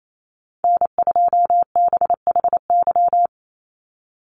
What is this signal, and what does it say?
Morse code, heard as a keyed tone, about 28 wpm, 700 Hz. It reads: N2BHY
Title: USA Call Signs 28wpm